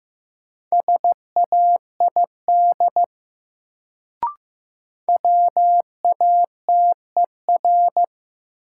Morse code, heard as a keyed tone, about 15 wpm, 700 Hz. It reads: SAID WATER
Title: Sentences from Top 100 Words - New Word Drill - 15wpm